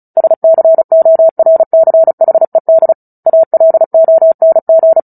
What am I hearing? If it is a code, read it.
Morse code, heard as a keyed tone, about 34 wpm, 650 Hz. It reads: SCORCHED ALONG
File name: Book Two / 09-Chapter 9 - 35 WPM